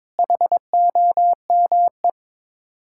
Morse code, heard as a keyed tone, about 22 wpm, 700 Hz. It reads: HOME
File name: practice / 22wpm / Sets of 8 Words - 22wpm